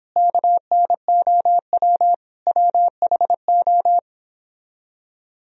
Morse code, heard as a keyed tone, about 26 wpm, 700 Hz. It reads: KNOW WHO
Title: Speed-Racing - Sets of 5 Words 22wpm